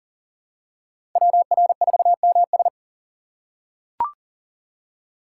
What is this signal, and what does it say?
Morse code, heard as a keyed tone, about 40 wpm, 700 Hz. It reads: WR4MS
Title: USA Call Signs - 5 Characters - 4x Character Spacing - 40wpm